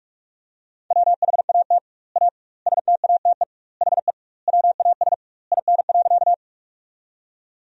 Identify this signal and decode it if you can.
Morse code, heard as a keyed tone, about 41 wpm, 700 Hz. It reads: WHAT A STATE HE WAS IN.
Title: 2x Word-Spacing - Sentences from Top 400 Words - 45wpm